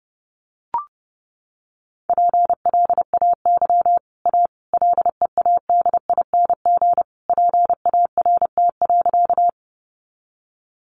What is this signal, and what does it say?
Morse code, heard as a keyed tone, about 30 wpm, 700 Hz. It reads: PLAY A LEADING PART.
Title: Sentences from Top 1100 Words 30wpm